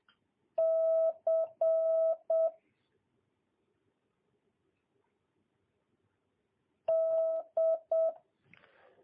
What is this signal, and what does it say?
Morse code, heard as a keyed tone, about 7 wpm, 650 Hz. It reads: C D
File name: jh8cbh / 2022cw